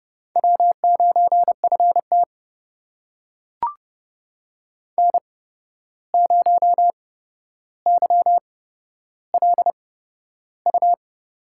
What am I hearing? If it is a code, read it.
Morse code, heard as a keyed tone, about 30 wpm, spaced as for 12 wpm, 700 Hz. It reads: W9FT N0YLU
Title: USA Call Signs - 8x Character Spacing - 30wpm